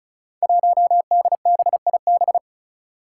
Morse code, heard as a keyed tone, about 34 wpm, 700 Hz. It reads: 1DBIB
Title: Five Challenging Letter-Number Combinations 35wpm